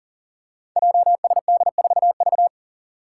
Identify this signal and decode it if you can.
Morse code, heard as a keyed tone, about 37 wpm, 700 Hz. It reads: JSD4V
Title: Five Challenging Letter-Number Combinations 40wpm